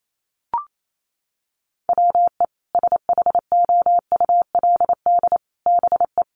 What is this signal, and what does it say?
Morse code, heard as a keyed tone, about 28 wpm, 700 Hz. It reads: WE SHOULD BE
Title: Sets of 5 Words Encore2 28wpm